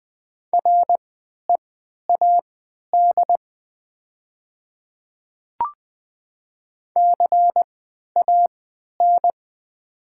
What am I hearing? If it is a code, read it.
Morse code, heard as a keyed tone, about 20 wpm, spaced as for 11 wpm, 700 Hz. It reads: READ CAN
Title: 3x Character Spacing - Top 200 Words - 20wpm